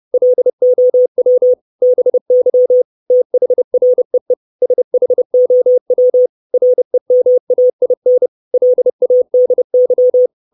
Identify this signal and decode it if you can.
Morse code, heard as a keyed tone, about 30 wpm, 500 Hz. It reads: LOW BY THREE SHOW REMAIN LADY
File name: 632 Words 30WPM